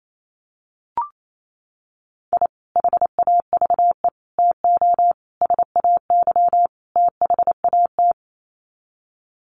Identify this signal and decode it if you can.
Morse code, heard as a keyed tone, about 28 wpm, 700 Hz. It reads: I HAVE TO SAY THAT
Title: Sets of 5 Words Encore 28wpm